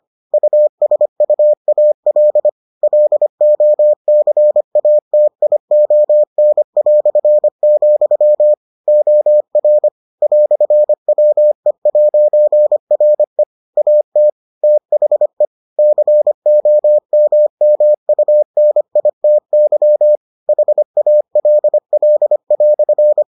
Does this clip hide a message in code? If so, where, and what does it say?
Yes: Morse code from 0.3 to 23.2 s, heard as a keyed tone, about 25 wpm, 600 Hz. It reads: USUAL LOCATION", OR "WE'RE AT THE COMMUNITY HALL"